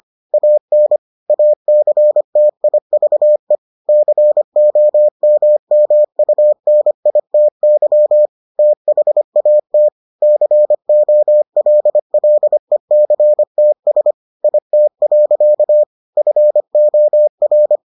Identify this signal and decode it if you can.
Morse code, heard as a keyed tone, about 25 wpm, 600 Hz. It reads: AN ACTIVE COMMUNITY THAT COLLECTS IT. FOR